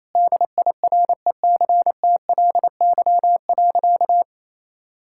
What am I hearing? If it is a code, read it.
Morse code, heard as a keyed tone, about 28 wpm, 700 Hz. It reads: DIRECTLY.
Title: Sentences from Top 1000 Words - Review Part 5 - 28wpm